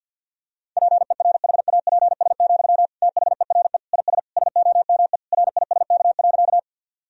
Morse code, heard as a keyed tone, about 50 wpm, 700 Hz. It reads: PERHAPS, THERE IS SOME RISK.
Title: Sentences from Top 1000 Words - Review Part 4 - 50wpm